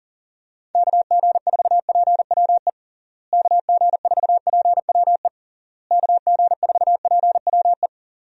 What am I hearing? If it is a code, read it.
Morse code, heard as a keyed tone, about 37 wpm, 700 Hz. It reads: KG4PWE KG4PWE KG4PWE
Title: USA Call Signs - 6 Characters - Repeated 3x - 40wpm